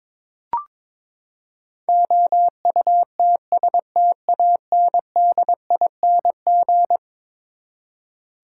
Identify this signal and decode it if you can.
Morse code, heard as a keyed tone, about 22 wpm, 700 Hz. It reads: OUTSTANDING
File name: practice / 22wpm / Eleven Letter Words - 22wpm